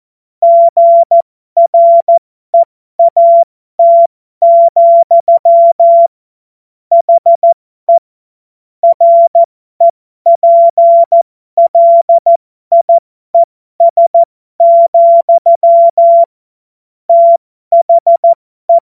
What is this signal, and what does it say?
Morse code, heard as a keyed tone, about 14 wpm, spaced as for 12 wpm, 680 Hz. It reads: GREAT, HE REPLIES, THE
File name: Morsekurs / Witze